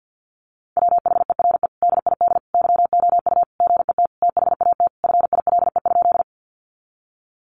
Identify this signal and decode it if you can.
Morse code, heard as a keyed tone, about 50 wpm, 700 Hz. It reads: WHERE DID YOU GET THAT FILE?
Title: Sentences from Top 1200 Words 50wpm